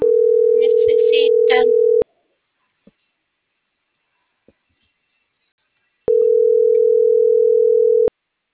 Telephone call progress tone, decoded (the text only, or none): ringback tone